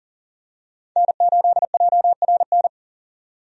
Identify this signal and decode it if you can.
Morse code, heard as a keyed tone, about 40 wpm, 700 Hz. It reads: N8JRN
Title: USA Call Signs - 5 Characters - 40wpm